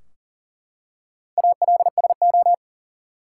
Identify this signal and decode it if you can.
Morse code, heard as a keyed tone, about 40 wpm, 700 Hz. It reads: ALSO
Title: Four Letter Words - 40wpm